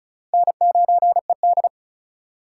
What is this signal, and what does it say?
Morse code, heard as a keyed tone, about 34 wpm, 700 Hz. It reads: N9ED
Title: USA Call Signs - 4 Characters - 4x Character Spacing - 35wpm